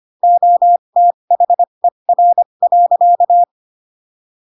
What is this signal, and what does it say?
Morse code, heard as a keyed tone, about 25 wpm, spaced as for 22 wpm, 700 Hz. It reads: OTHER.
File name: Pinocchio / Pinocchio ch2 18 25 ik1hge